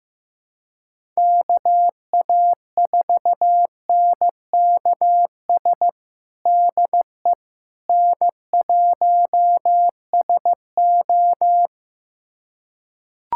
Morse code, heard as a keyed tone, about 15 wpm, 700 Hz. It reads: KA4NKS DE N1SO
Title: USA Callsign DE Callsign 15wpm